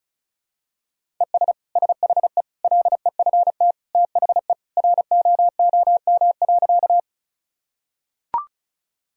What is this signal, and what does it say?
Morse code, heard as a keyed tone, about 35 wpm, 700 Hz. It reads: ES SHE LEFT THE ROOM.
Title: Sentences from Top 300 Words and Abbreviations 35wpm